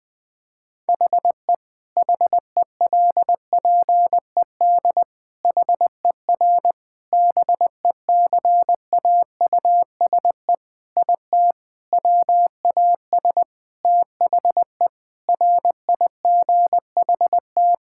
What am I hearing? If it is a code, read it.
Morse code, heard as a keyed tone, about 20 wpm, 700 Hz. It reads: HE HELPED HER BECAUSE IT WAS THE RIGHT
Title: Sentences with 8-Word Phrases - 20wpm